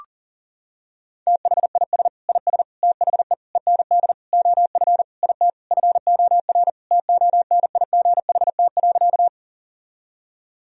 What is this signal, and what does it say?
Morse code, heard as a keyed tone, about 37 wpm, 700 Hz. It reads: THIS IS THE END OF IT FOR TONIGHT.
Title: Sentences from Top 1000 Words - Review Part 3 - 40wpm